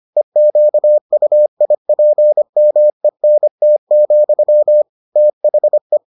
Morse code, heard as a keyed tone, about 25 wpm, 600 Hz. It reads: EQUIPMENT, THE